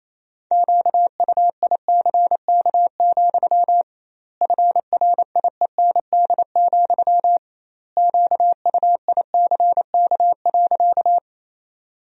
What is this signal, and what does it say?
Morse code, heard as a keyed tone, about 27 wpm, 700 Hz. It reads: QUICK, FRIEND, QUICK.
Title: 2x Word-Spacing - Sentences from Top 500 Words - 28wpm